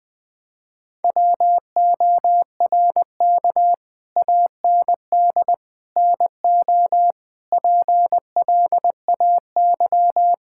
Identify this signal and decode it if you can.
Morse code, heard as a keyed tone, about 20 wpm, 700 Hz. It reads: WORK AND NO PLAY
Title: Sets of 8 Words - 20wpm